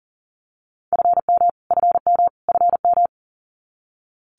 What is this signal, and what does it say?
Morse code, heard as a keyed tone, about 38 wpm, 700 Hz. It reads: FM FM FM
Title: Abbreviations Repeated 3x - 40wpm